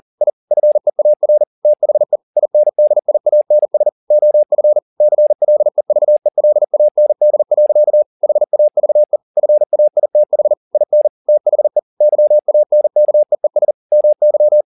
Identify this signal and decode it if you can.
Morse code, heard as a keyed tone, about 40 wpm, 600 Hz. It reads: I FEAR THE INDIANS OF CLEVELAND. HAVE FAITH IN THE YANKEES MY